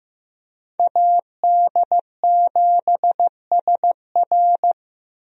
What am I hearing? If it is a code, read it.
Morse code, heard as a keyed tone, about 15 wpm, 700 Hz. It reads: AD7SR
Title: USA Call Signs - 5 Characters - Repeated 3x - 15wpm